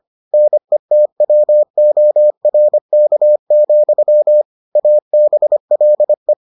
Morse code, heard as a keyed tone, about 25 wpm, 600 Hz. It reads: NETWORK, ABLE